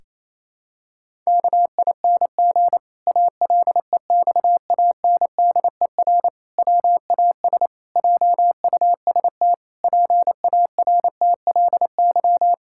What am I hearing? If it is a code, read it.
Morse code, heard as a keyed tone, about 28 wpm, 700 Hz. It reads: KING ALEXANDER WAS JUST PARTLY
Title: Pangram Challenge v2 - 28wpm